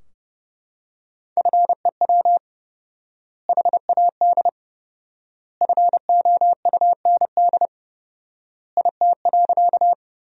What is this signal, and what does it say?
Morse code, heard as a keyed tone, about 30 wpm, 700 Hz. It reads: FEW HAD FOUND IT.
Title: 4x Word-Spacing - Sentences from Top 300 Words - 30wpm